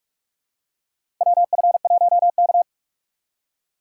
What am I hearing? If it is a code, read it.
Morse code, heard as a keyed tone, about 46 wpm, 700 Hz. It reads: WF1K
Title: USA Call Signs - 4 Characters - 8x Character Spacing - 45wpm